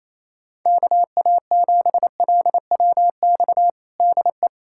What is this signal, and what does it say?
Morse code, heard as a keyed tone, about 28 wpm, 700 Hz. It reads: KA7LWX DE
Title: USA Callsign DE Callsign - 6 Characters - 28wpm